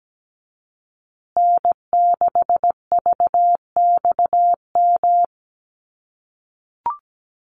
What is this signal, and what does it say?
Morse code, heard as a keyed tone, about 17 wpm, 700 Hz. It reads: N6VXM
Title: USA Call Signs Repeated 3x 17wpm